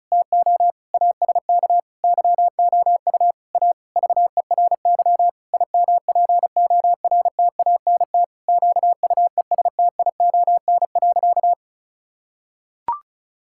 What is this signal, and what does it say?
Morse code, heard as a keyed tone, about 36 wpm, 700 Hz. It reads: TO ASK YOU A VERY IMPORTANT QUESTION.
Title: Sentences from Top 1000 Words - Review Part 6 - 35wpm